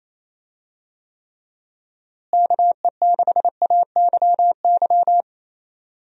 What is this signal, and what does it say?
Morse code, heard as a keyed tone, about 28 wpm, 700 Hz. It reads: KE6AYY
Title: USA Call Signs - 6 Characters - Repeated 2x - 28wpm